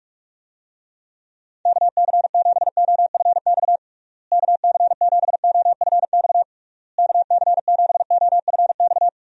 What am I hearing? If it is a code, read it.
Morse code, heard as a keyed tone, about 41 wpm, 700 Hz. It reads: KC7OFX KC7OFX KC7OFX
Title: USA Call Signs - 6 Characters - Repeated 3x - 45wpm